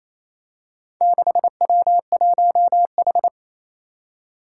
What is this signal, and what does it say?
Morse code, heard as a keyed tone, about 28 wpm, 700 Hz. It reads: 6W1H
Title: Four Challenging Letter-Number Combinations 28wpm